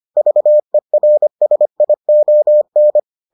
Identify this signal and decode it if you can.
Morse code, heard as a keyed tone, about 25 wpm, 600 Hz. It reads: VERSION